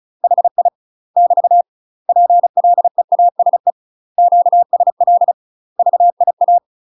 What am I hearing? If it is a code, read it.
Morse code, heard as a keyed tone, about 34 wpm, 700 Hz. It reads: HI <BT> PLEASE QSL VIA
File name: UK MASTER QSOs TRAINING / 35WPM / UK MASTER QSOs 0019